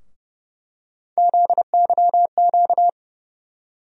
Morse code, heard as a keyed tone, about 29 wpm, 700 Hz. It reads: ZYQ
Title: USA Call Sign Postfixes - 3 Letters 30wpm